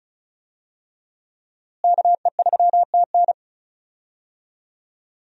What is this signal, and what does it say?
Morse code, heard as a keyed tone, about 36 wpm, 700 Hz. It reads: KE3TN
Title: USA Call Signs - 5 Characters - Repeated 3x - 35wpm